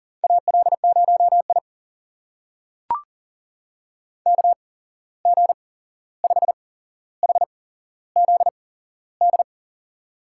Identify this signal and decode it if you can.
Morse code, heard as a keyed tone, about 40 wpm, spaced as for 15 wpm, 700 Hz. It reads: AL0I KG5HZD
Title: USA Call Signs - 8x Character Spacing - 40wpm